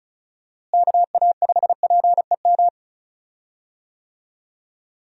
Morse code, heard as a keyed tone, about 34 wpm, 700 Hz. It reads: KA5PEM
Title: USA Call Signs - 6 Characters - Repeated 2x - 35wpm